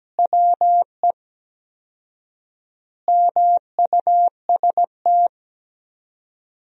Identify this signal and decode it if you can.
Morse code, heard as a keyed tone, about 17 wpm, 700 Hz. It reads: WE MUST
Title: 4x Word-Spacing - Sentences from Top 500 Words - 17wpm